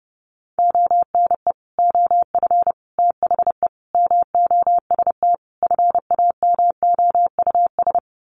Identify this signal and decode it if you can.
Morse code, heard as a keyed tone, about 30 wpm, 700 Hz. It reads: ONE OF THE MOST FAMOUS
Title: Sets of 5 Words Encore2 30wpm